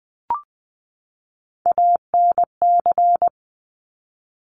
Morse code, heard as a keyed tone, about 20 wpm, 700 Hz. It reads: ANC
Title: Top Three Letter Combinations in Words 20wpm